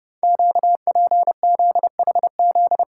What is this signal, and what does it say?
Morse code, heard as a keyed tone, about 30 wpm, 690 Hz. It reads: QPZHZ
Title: Code Groups - 5 Letters - 30wpm